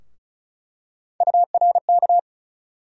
Morse code, heard as a keyed tone, about 34 wpm, 700 Hz. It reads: URK
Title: USA Call Sign Postfixes - 3 Letters 35wpm